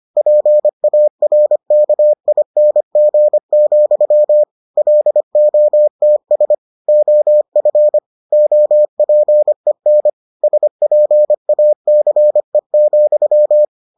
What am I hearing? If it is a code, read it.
Morse code, heard as a keyed tone, about 25 wpm, 600 Hz. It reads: PARKING, LOTS OF OPEN SPACE,